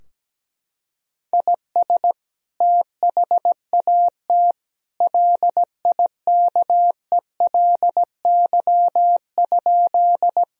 Morse code, heard as a keyed tone, about 17 wpm, 700 Hz. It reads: IS THAT LIKELY?
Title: Sentences from Top 1000 Words - Review Part 4 - 17wpm